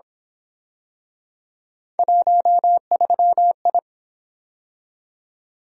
Morse code, heard as a keyed tone, about 27 wpm, 700 Hz. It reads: 13I
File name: Speed-Racing - Three Letter-Number Combinations - 22wpm